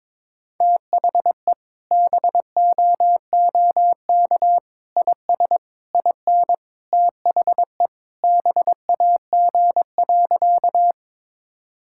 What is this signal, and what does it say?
Morse code, heard as a keyed tone, about 22 wpm, 700 Hz. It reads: THE BOOK IS IN THE BAG.
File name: practice / 22wpm / Sentences with 3-Word Phrases - 22wpm